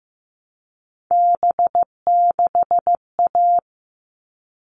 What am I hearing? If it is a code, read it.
Morse code, heard as a keyed tone, about 15 wpm, 700 Hz. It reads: B6A
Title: All Cut Numbers - 3-Digit 15wpm